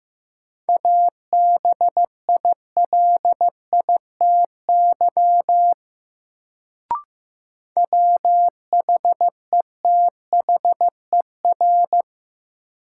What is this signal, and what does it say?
Morse code, heard as a keyed tone, about 15 wpm, 700 Hz. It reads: ABILITY WHETHER
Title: Sets of 3 Words - New Word Drill - 15wpm